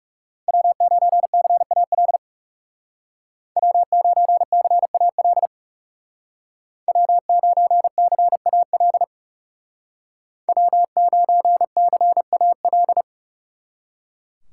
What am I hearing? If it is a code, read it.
Morse code, heard as a keyed tone, about 37 wpm, 700 Hz. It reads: W9CAL W9CAL W9CAL W9CAL
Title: Speed Racing - USA Call Signs 30wpm